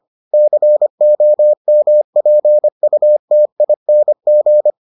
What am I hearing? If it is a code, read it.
Morse code, heard as a keyed tone, about 25 wpm, 600 Hz. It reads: COMPUTING